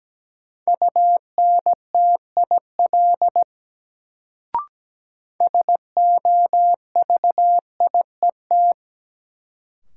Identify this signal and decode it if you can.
Morse code, heard as a keyed tone, about 17 wpm, 700 Hz. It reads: UNTIL SOVIET
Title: Sets of 5 Words Encore - New Word Drill - 17wpm